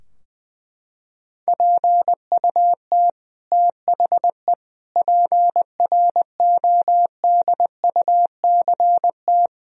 Morse code, heard as a keyed tone, about 20 wpm, 700 Hz. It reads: PUT THE PRODUCT
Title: Sentences from Top 1000 Words - Review Part 2 - 20wpm